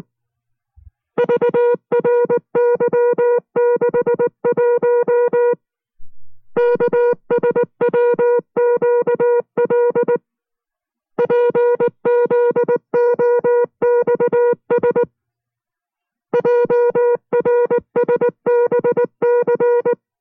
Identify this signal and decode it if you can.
Morse code, heard as a keyed tone, about 19 wpm, 480 Hz. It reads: VRY61 KSWQL PZOXS JRSBC